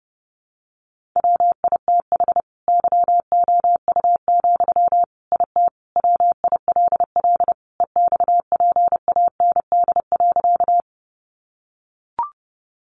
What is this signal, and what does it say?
Morse code, heard as a keyed tone, about 30 wpm, 700 Hz. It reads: WITH YOU, IT WILL EXPAND.
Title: Sentences from Top 1400 Words 30wpm